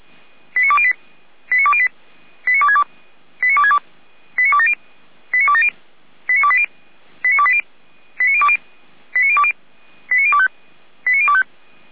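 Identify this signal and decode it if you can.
Five-tone selective call (ZVEI), heard as quick runs of five tones, tones of about 70 ms. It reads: 89208 89208 89262 89262 89270 89270 89280 89280 89020 89020 89025 89025
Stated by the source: ZVEI five-tone selective call received on an SDR.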